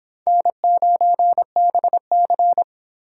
Morse code, heard as a keyed tone, about 24 wpm, 700 Hz. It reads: N9BC